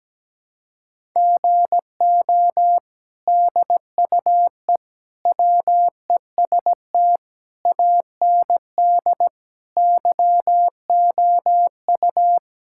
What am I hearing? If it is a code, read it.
Morse code, heard as a keyed tone, about 17 wpm, 700 Hz. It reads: GO DUE WEST AND YOU
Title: Sentences from Top 1000 Words - Review Part 6 - 17wpm